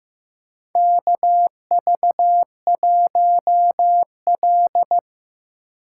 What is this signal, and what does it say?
Morse code, heard as a keyed tone, about 16 wpm, 700 Hz. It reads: KV1L
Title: USA Call Signs - 2x Character Spacing - 15wpm